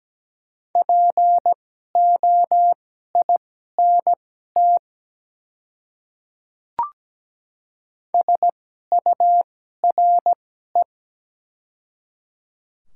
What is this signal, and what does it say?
Morse code, heard as a keyed tone, about 17 wpm, spaced as for 12 wpm, 690 Hz. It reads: POINT SURE
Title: 2x Character Spacing - Top 300 Words - 17wpm